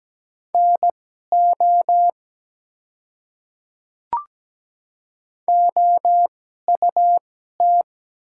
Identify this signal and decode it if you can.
Morse code, heard as a keyed tone, about 17 wpm, spaced as for 12 wpm, 700 Hz. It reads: NO OUT
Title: 2x Character Spacing - Top 100 Words - 17wpm